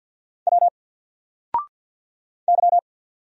Morse code, heard as a keyed tone, about 50 wpm, 700 Hz. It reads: W Y
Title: Single Letter-Number Alt 50wpm